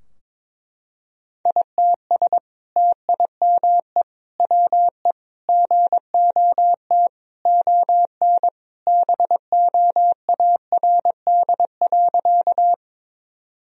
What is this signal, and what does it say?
Morse code, heard as a keyed tone, about 21 wpm, 700 Hz. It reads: ITS TIME WE GOT ON BOARD.
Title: Sentences from Top 1000 Words - Review Part 4 - 22wpm